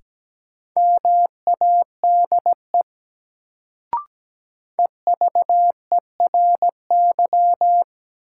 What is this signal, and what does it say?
Morse code, heard as a keyed tone, about 17 wpm, 700 Hz. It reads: MADE EVERY
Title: Sentences from Top 200 Words - New Word Drill - 17wpm